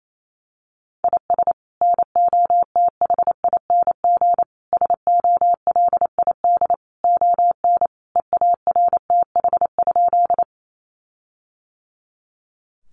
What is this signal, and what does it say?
Morse code, heard as a keyed tone, about 28 wpm, 700 Hz. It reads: IS NOTHING SOLID ON EARTH?
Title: Sentences from Top 1500 Words 28wpm